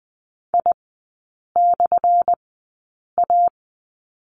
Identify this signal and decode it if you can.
Morse code, heard as a keyed tone, about 20 wpm, spaced as for 8 wpm, 700 Hz. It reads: I/A
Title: 178 Lesson 29e - Slash - Three Characters 20wpm